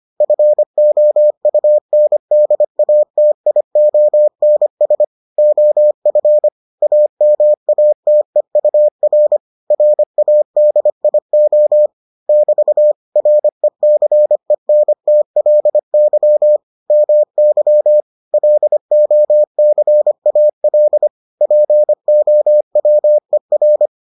Morse code, heard as a keyed tone, about 25 wpm, 600 Hz. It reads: FOUNDATIONS OF AMATEUR RADIO <BT> RECENTLY MY LOCAL POWER